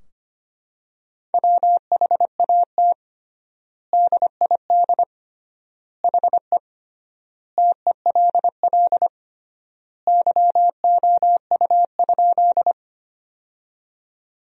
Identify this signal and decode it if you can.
Morse code, heard as a keyed tone, about 25 wpm, 700 Hz. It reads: WHAT DID HE TELL YOU?
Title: 3x Word-Spacing - Sentences from Top 200 Words - 25wpm